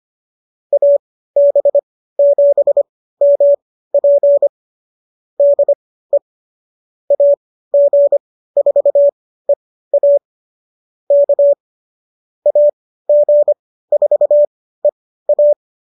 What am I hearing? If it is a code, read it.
Morse code, heard as a keyed tone, about 25 wpm, spaced as for 15 wpm, 570 Hz. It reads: AB7MP DE AG4EA K AG4EA